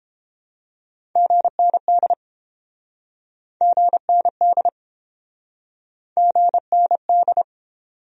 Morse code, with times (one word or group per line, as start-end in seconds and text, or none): 1.2-2.1 GND
3.6-4.7 GND
6.2-7.4 GND